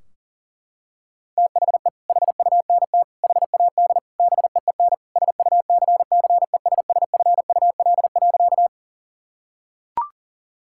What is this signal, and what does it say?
Morse code, heard as a keyed tone, about 40 wpm, 700 Hz. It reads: THE HUNT HAD BEEN SUCCESSFUL.
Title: Sentences from Top 1600 Words 40wpm